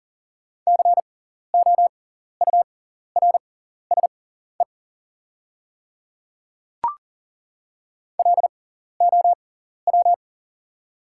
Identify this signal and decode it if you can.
Morse code, heard as a keyed tone, about 40 wpm, spaced as for 14 wpm, 700 Hz. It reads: COURSE LOW
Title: 6x Character Spacing - Top 500 Words - 40wpm